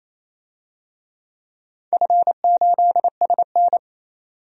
Morse code, heard as a keyed tone, about 27 wpm, 700 Hz. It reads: F8SN
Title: DXCC-100 Call Signs - 4 Characters - 28wpm